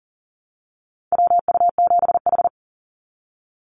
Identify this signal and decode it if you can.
Morse code, heard as a keyed tone, about 42 wpm, 700 Hz. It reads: WU7H
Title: NA SOTA Activator Callsigns 40wpm